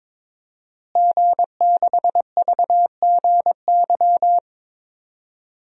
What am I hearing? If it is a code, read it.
Morse code, heard as a keyed tone, about 22 wpm, 700 Hz. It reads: G6VGY
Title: DXCC-100 Call Signs - 5 Characters - Repeated 2x - 22wpm